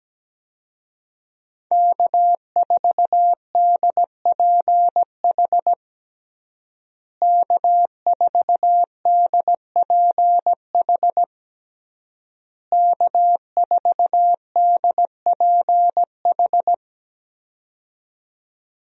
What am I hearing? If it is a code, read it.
Morse code, heard as a keyed tone, about 17 wpm, 700 Hz. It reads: K4DPH K4DPH K4DPH
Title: USA Call Signs - 5 Characters - Repeated 3x - 17wpm